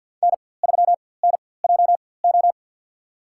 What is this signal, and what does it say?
Morse code, heard as a keyed tone, about 48 wpm, spaced as for 23 wpm, 700 Hz. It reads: N3NJO
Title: USA Call Signs - 5 Characters - 4x Character Spacing - 50wpm